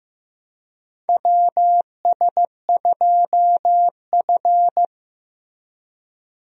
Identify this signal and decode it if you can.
Morse code, heard as a keyed tone, about 16 wpm, 700 Hz. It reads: WS2F
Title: USA Call Signs - 4 Characters - Repeated 3x - 15wpm